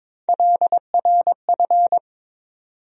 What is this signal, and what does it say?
Morse code, heard as a keyed tone, about 21 wpm, 700 Hz. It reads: LRF
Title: USA Call Sign Postfixes - 3 Letters 22wpm